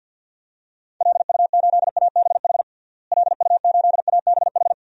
Morse code, heard as a keyed tone, about 52 wpm, 700 Hz. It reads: LU8ABH LU8ABH
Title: DXCC-100 Call Signs - 6 Characters - Repeated 2x - 50wpm